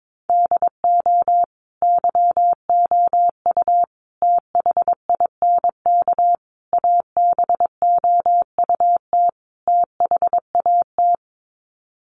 Morse code, heard as a keyed tone, about 22 wpm, 700 Hz. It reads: DO YOU THINK ABOUT THAT
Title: Sets of 5 Words Encore2 22wpm